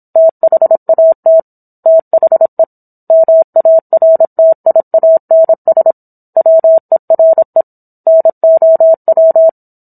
Morse code, heard as a keyed tone, about 27 wpm, 650 Hz. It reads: THAT THE MARTIANS WERE NOW